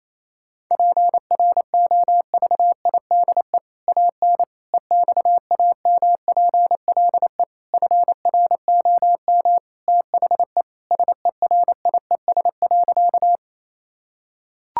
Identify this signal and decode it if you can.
Morse code, heard as a keyed tone, about 28 wpm, 700 Hz. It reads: PROVIDE AN EXAMPLE FROM THE SERIES.
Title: Sentences from Top 1000 Words - Review Part 5 - 28wpm